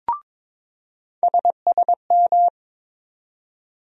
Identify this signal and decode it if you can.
Morse code, heard as a keyed tone, about 22 wpm, 700 Hz. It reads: SSM
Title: USA Call Sign Postfixes - 3 Letters 22wpm